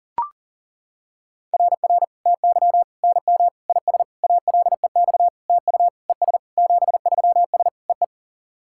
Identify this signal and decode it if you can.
Morse code, heard as a keyed tone, about 40 wpm, 700 Hz. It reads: RR TY NM IS ALEX TU ES 73S EE
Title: SOTA QSO Practice 40wpm